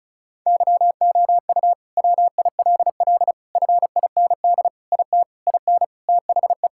Morse code, heard as a keyed tone, about 36 wpm, 700 Hz. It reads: YOU WILL FIND IT IN THE
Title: Sentences from Top 1000 Words - Review Part 5 - 35wpm